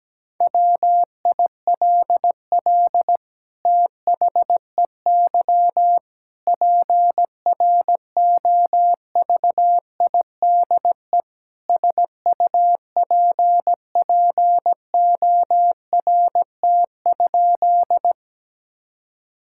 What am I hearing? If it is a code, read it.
Morse code, heard as a keyed tone, about 17 wpm, 700 Hz. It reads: WILL THEY PROVIDE SUPPORT?
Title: Sentences from Top 1000 Words - Review Part 5 - 17wpm